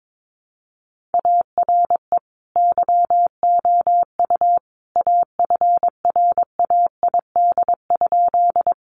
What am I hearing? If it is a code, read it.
Morse code, heard as a keyed tone, about 22 wpm, 700 Hz. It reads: ARE YOU AFRAID?
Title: Sentences from Top 1500 Words 22wpm